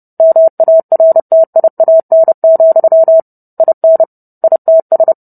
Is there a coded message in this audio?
Yes: Morse code from 0.2 to 5.1 s, heard as a keyed tone, about 30 wpm, 650 Hz. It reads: MARTIAN, IN ITS